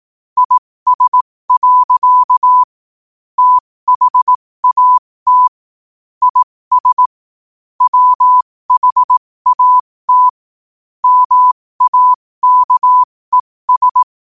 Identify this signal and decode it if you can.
Morse code, heard as a keyed tone, about 18 wpm, spaced as for 15 wpm, 1000 Hz. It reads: IS. THAT IS WHAT MAKES
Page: qotd-at-15-wpm